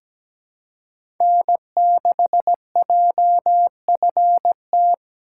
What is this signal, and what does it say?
Morse code, heard as a keyed tone, about 17 wpm, 700 Hz. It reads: N6JFT
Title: USA Call Signs - 5 Characters - 6x Character Spacing - 17wpm